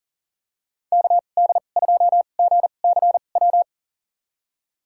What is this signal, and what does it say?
Morse code, heard as a keyed tone, about 40 wpm, spaced as for 29 wpm, 700 Hz. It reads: KD2GCW
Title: USA Call Signs - 2x Character Spacing - 40wpm